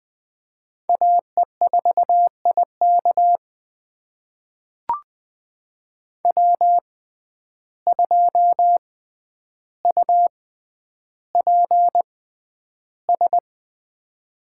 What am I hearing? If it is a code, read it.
Morse code, heard as a keyed tone, about 20 wpm, spaced as for 10 wpm, 700 Hz. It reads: AE4IK W2UPS
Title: USA Call Signs - 6x Character Spacing - 20wpm